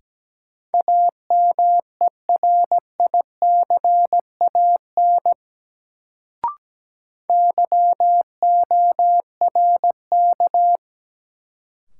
Morse code, heard as a keyed tone, about 17 wpm, 700 Hz. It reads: AMERICAN YORK
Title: Sets of 2 Words - New Word Drill - 17wpm